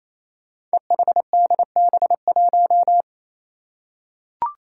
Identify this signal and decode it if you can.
Morse code, heard as a keyed tone, about 28 wpm, 700 Hz. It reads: EHDB1
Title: Five Challenging Letter-Number Combinations 28wpm